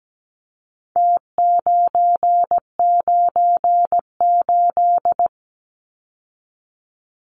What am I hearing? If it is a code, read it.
Morse code, heard as a keyed tone, about 17 wpm, 700 Hz. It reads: T998
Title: Numbers - Contest Style - 4-Digit 17wpm